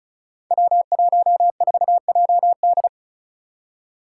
Morse code, heard as a keyed tone, about 36 wpm, 700 Hz. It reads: W14JD
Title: Five Challenging Letter-Number Combinations 35wpm